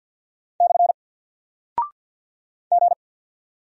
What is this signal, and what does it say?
Morse code, heard as a keyed tone, about 48 wpm, 700 Hz. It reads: / G
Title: Single Character (Letter:Number:Punctuation:Prosign) Alt 50wpm